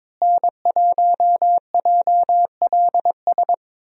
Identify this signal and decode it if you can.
Morse code, heard as a keyed tone, about 22 wpm, 700 Hz. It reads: N1JLS
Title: USA Call Signs - 5 Characters - Repeated 2x - 22wpm